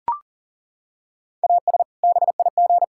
Morse code, heard as a keyed tone, about 40 wpm, 700 Hz. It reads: AS BIG
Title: Sentences from Top 1600 Words 40wpm